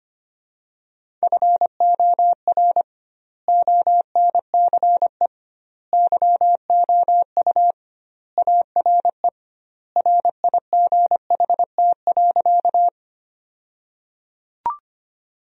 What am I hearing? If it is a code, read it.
Morse code, heard as a keyed tone, about 25 wpm, 700 Hz. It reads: FOR ONCE YOU ARE RIGHT.
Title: 2x Word-Spacing - Sentences from Top 300 Words - 25wpm